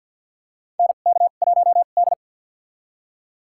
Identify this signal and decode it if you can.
Morse code, heard as a keyed tone, about 53 wpm, spaced as for 37 wpm, 700 Hz. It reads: NK1D
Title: USA Call Signs - 4 Characters - 2x Character Spacing - 50wpm